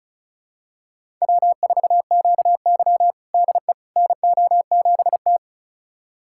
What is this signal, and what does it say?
Morse code, heard as a keyed tone, about 36 wpm, 700 Hz. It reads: W4QY DE NO7T
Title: USA Callsign DE Callsign - 4 Characters - 35wpm